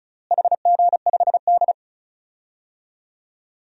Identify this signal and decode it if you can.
Morse code, heard as a keyed tone, about 36 wpm, 700 Hz. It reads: HG5D
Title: DXCC-100 Call Signs - 4 Characters - Repeated 2x - 35wpm